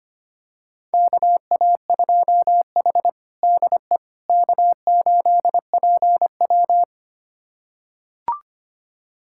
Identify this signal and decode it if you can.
Morse code, heard as a keyed tone, about 25 wpm, 700 Hz. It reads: KA2H DE K8PW
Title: USA Callsign DE Callsign - 4 Characters - 25wpm